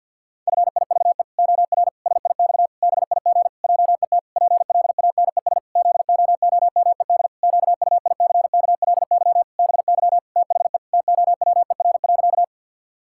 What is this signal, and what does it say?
Morse code, heard as a keyed tone, about 53 wpm, 700 Hz. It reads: FIVE OR SIX BIG JET PLANES ZOOMED QUICKLY BY THE TOWER.
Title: Pangram Challenge v2 - 50wpm